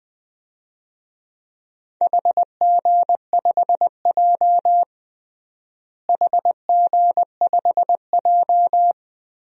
Morse code, heard as a keyed tone, about 20 wpm, 700 Hz. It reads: HG5J HG5J
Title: DXCC-100 Call Signs - 4 Characters - Repeated 3x - 20wpm